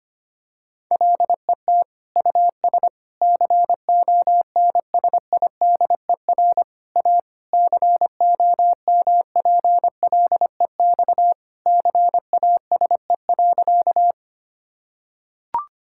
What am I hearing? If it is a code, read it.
Morse code, heard as a keyed tone, about 25 wpm, 700 Hz. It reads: LET US CONSIDER A COMPLEX CASE.
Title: Sentences from Top 1000 Words - Review Part 6 - 25wpm